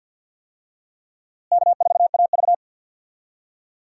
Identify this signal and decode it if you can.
Morse code, heard as a keyed tone, about 52 wpm, 700 Hz. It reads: K4AV
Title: USA Call Signs - 4 Characters - Repeated 3x - 50wpm